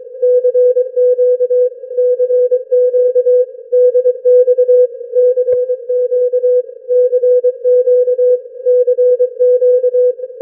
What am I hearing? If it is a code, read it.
Morse code, heard as a keyed tone, about 23 wpm, 500 Hz. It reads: CQ CQ DX CQ CQ CQ
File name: n6tr / n6tr 7 25dec2007atOZ1LXJ